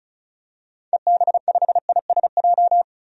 Morse code, heard as a keyed tone, about 34 wpm, 700 Hz. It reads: EB5ISJ
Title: DXCC-100 Call Signs - 6 Characters - 35wpm